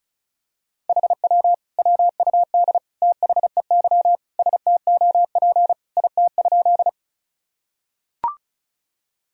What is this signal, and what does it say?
Morse code, heard as a keyed tone, about 36 wpm, 700 Hz. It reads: HW WUD THEY STOP IT?
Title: Sentences from Top 300 Words and Abbreviations 35wpm